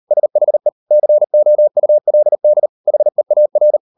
Morse code, heard as a keyed tone, about 38 wpm, 600 Hz. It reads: SHE COULD HEAR